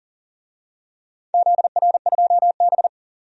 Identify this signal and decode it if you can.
Morse code, heard as a keyed tone, about 37 wpm, 700 Hz. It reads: ZR2B
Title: DXCC-100 Call Signs - 4 Characters - 40wpm